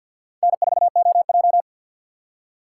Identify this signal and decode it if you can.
Morse code, heard as a keyed tone, about 50 wpm, 700 Hz. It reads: N4OJ
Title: USA Call Signs - 3x Character Spacing - 50wpm